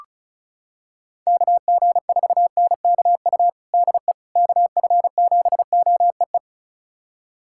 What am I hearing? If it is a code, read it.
Morse code, heard as a keyed tone, about 36 wpm, 700 Hz. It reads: KG4NKU DE KF7OEE
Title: USA Callsign DE Callsign - 6 Characters - 35wpm